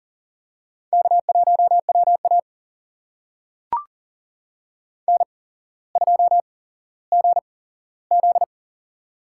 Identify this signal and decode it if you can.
Morse code, heard as a keyed tone, about 40 wpm, spaced as for 17 wpm, 700 Hz. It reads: K1WA N2GZ
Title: USA Call Signs - 4 Characters - 8x Character Spacing - 40wpm